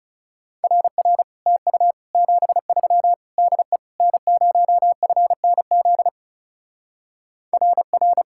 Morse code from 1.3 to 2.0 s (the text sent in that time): TU